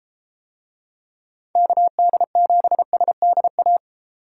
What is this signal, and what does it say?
Morse code, heard as a keyed tone, about 32 wpm, 700 Hz. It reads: KD7SDA